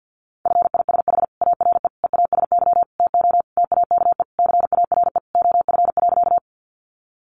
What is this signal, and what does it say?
Morse code, heard as a keyed tone, about 48 wpm, 700 Hz. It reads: FISH ARE EASY TO TAKE CARE OF.
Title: Additional Easy Phrases 50wpm